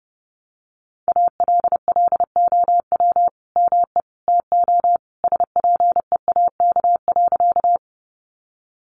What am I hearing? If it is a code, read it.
Morse code, heard as a keyed tone, about 28 wpm, 700 Hz. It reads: ALLOW ME TO SPEAK.
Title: Sentences from Top 1100 Words 30wpm